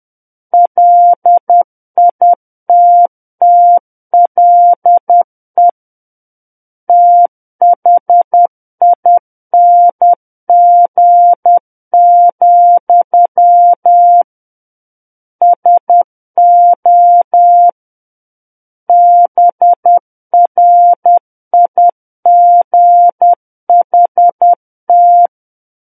Morse code, heard as a keyed tone, about 10 wpm, 700 Hz. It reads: LITTLE THING, SO BRIGHT
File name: Book One / 01-Chapter 1 - 10 WPM